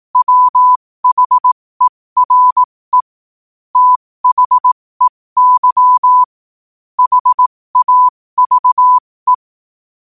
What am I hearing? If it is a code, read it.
Morse code, heard as a keyed tone, about 18 wpm, spaced as for 15 wpm, 1000 Hz. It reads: WHERE THEY HAVE
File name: qotdfiles / 15wpm